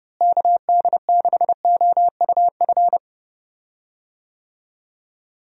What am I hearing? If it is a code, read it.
Morse code, heard as a keyed tone, about 30 wpm, 700 Hz. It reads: KD6OUF
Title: USA Call Signs - 6 Characters - Repeated 3x - 30wpm